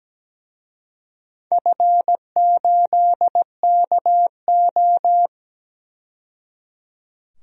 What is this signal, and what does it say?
Morse code, heard as a keyed tone, about 17 wpm, 700 Hz. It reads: F8KO
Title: DXCC-100 Call Signs - 4 Characters - 17wpm